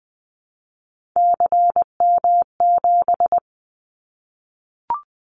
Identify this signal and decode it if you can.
Morse code, heard as a keyed tone, about 19 wpm, 700 Hz. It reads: CM7
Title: Callsign Prefixes for Top 100 DXCC - 20wpm